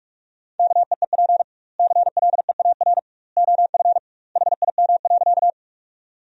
Morse code, heard as a keyed tone, about 46 wpm, 700 Hz. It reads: KEEP CLEAR OF HIM.
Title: 2x Word-Spacing - Sentences from Top 500 Words - 45wpm